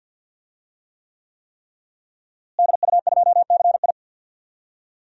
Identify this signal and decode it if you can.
Morse code, heard as a keyed tone, about 47 wpm, 700 Hz. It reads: DU2CI
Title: DXCC-100 Call Signs - 5 Characters - Repeated 3x - 50wpm